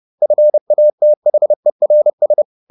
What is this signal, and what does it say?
Morse code, heard as a keyed tone, about 30 wpm, 600 Hz. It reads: FATHERS
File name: the old man and the sea / the old man and the sea 1 30wpm